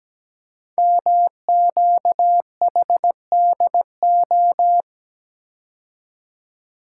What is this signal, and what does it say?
Morse code, heard as a keyed tone, about 17 wpm, 700 Hz. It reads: MQHDO
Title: Code Groups - 5 Letters - 17wpm